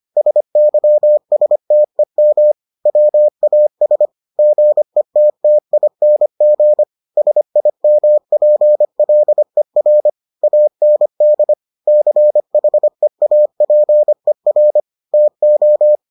Morse code, heard as a keyed tone, about 25 wpm, 600 Hz. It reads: SYSTEM WAS GETTING SIMPLER AND CHEAPER TO